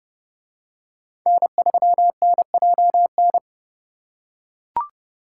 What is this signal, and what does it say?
Morse code, heard as a keyed tone, about 29 wpm, 700 Hz. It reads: N3NJN